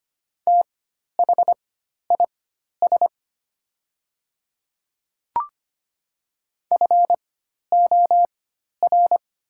Morse code, heard as a keyed tone, about 25 wpm, spaced as for 12 wpm, 700 Hz. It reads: THIS FOR
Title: 4x Character Spacing - Top 200 Words - 25wpm